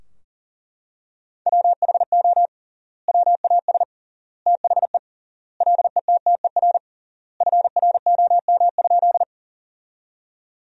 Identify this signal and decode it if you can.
Morse code, heard as a keyed tone, about 40 wpm, 700 Hz. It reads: WHO WAS THE LETTER FROM?
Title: 3x Word-Spacing - Sentences from Top 300 Words - 40wpm